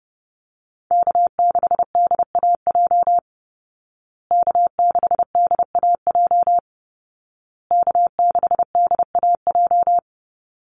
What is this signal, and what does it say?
Morse code, heard as a keyed tone, about 30 wpm, 700 Hz. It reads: K6DAJ K6DAJ K6DAJ
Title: Call Signs Repeated 2x - 30wpm